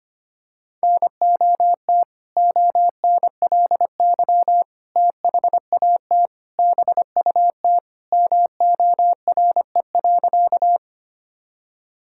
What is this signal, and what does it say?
Morse code, heard as a keyed tone, about 25 wpm, 700 Hz. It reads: NOT ONLY THAT BUT MORE.
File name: practice / 25wpm / Sentences with 2-Word Phrases - 25wpm